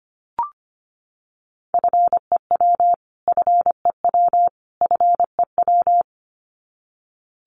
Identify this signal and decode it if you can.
Morse code, heard as a keyed tone, about 25 wpm, 700 Hz. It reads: FEW FEW FEW
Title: Top 500 Words Repeated 25wpm